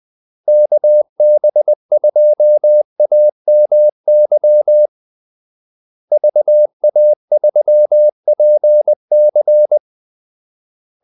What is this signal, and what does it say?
Morse code, heard as a keyed tone, about 20 wpm, 600 Hz. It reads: KB2AMY VA3PC
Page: Kaldesignaler 20 wpm